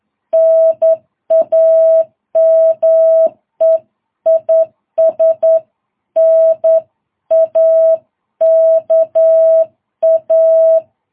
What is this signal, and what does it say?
Morse code, heard as a keyed tone, about 10 wpm, 660 Hz. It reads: NAME IS NAKA
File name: jh8cbh / 2022cw